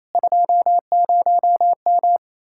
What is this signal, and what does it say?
Morse code, heard as a keyed tone, about 37 wpm, 700 Hz. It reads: 20M